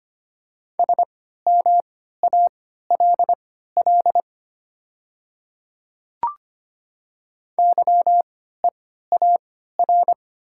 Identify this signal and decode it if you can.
Morse code, heard as a keyed tone, about 25 wpm, spaced as for 14 wpm, 700 Hz. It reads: SMALL YEAR
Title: 3x Character Spacing - Top 300 Words - 25wpm